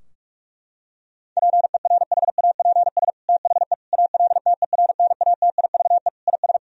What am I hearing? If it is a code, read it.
Morse code, heard as a keyed tone, about 46 wpm, 700 Hz. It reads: PERHAPS THE ALTERNATIVE IS
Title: Sentences from Top 1000 Words - Review Part 4 - 45wpm